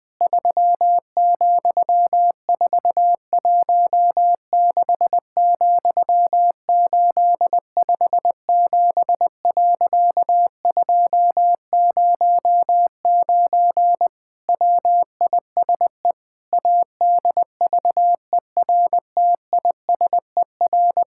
Morse code, heard as a keyed tone, about 20 wpm, 700 Hz. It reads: 3,416,857.209 WISE ADVERTISER
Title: Pangram Challenge v2 - 20wpm